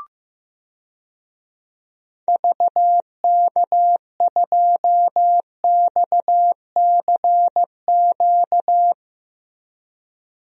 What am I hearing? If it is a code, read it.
Morse code, heard as a keyed tone, about 16 wpm, 700 Hz. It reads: VK2XCQ
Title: DXCC-100 Call Signs - 6 Characters - Repeated 3x - 15wpm